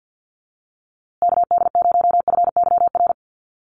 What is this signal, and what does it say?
Morse code, heard as a keyed tone, about 50 wpm, 700 Hz. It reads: KD0FYR
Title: USA Call Signs Repeated 2x 50wpm